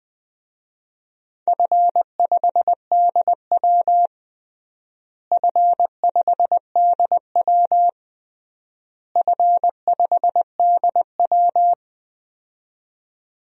Morse code, heard as a keyed tone, about 20 wpm, 700 Hz. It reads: F5DW F5DW F5DW
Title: DXCC-100 Call Signs - 4 Characters - Repeated 3x - 20wpm